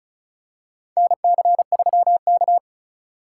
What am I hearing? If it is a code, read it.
Morse code, heard as a keyed tone, about 36 wpm, 700 Hz. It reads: NC3K
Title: USA Call Signs - 4 Characters - 2x Character Spacing - 35wpm